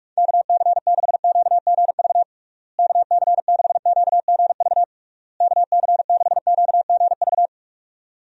Morse code, heard as a keyed tone, about 46 wpm, 700 Hz. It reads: KC6QGV KC6QGV KC6QGV
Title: USA Call Signs - 6 Characters - Repeated 3x - 45wpm